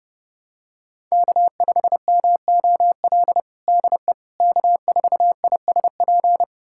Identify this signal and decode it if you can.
Morse code, heard as a keyed tone, about 30 wpm, 700 Hz. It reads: K5MOL DE K4ISP
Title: USA Callsign DE Callsign - 5 Characters - 30wpm